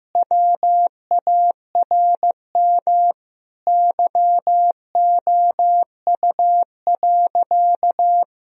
Morse code, heard as a keyed tone, about 15 wpm, 700 Hz. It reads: WARM YOU.
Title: Sentences from Top 1000 Words - Review Part 2 - 15wpm